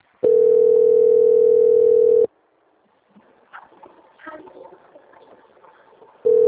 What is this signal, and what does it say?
Ringback tone (North America), 440 + 480 Hz.